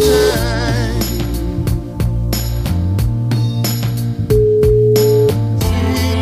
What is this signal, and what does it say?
Ringback tone (Europe, 425 Hz), with noise.